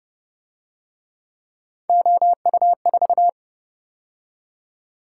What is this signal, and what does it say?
Morse code, heard as a keyed tone, about 31 wpm, 700 Hz. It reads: OU4
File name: Speed-Racing - Three Letter-Number Combinations - 25wpm